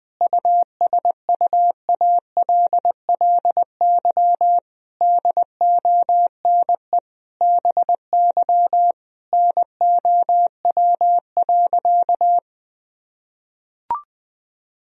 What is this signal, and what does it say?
Morse code, heard as a keyed tone, about 20 wpm, 700 Hz. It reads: USUALLY DONE BY NOW.
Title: Sentences from Top 1000 Words - Review Part 3 - 20wpm